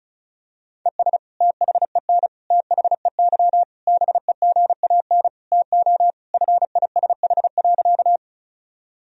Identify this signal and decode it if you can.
Morse code, heard as a keyed tone, about 36 wpm, 700 Hz. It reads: ES THEN THEY BEGAN TO FISH.
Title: Sentences from Top 300 Words and Abbreviations 35wpm